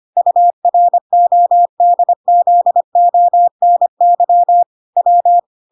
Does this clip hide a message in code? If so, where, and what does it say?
Yes: Morse code from 0.2 to 5.4 s, heard as a keyed tone, about 25 wpm, 700 Hz. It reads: URODZONY W